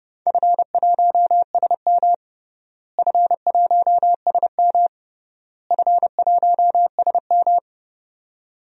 Morse code, heard as a keyed tone, about 32 wpm, 700 Hz. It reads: F1SM F1SM F1SM
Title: DXCC-100 Call Signs - 4 Characters - Repeated 3x - 30wpm